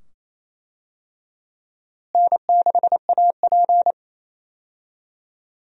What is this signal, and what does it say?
Morse code, heard as a keyed tone, about 28 wpm, 700 Hz. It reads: N6AP
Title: USA Call Signs - 4 Characters - Repeated 2x - 28wpm